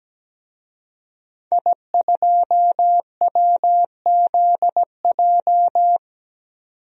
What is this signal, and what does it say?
Morse code, heard as a keyed tone, about 17 wpm, 700 Hz. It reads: I2WZJ
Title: DXCC-100 Call Signs - 5 Characters - 17wpm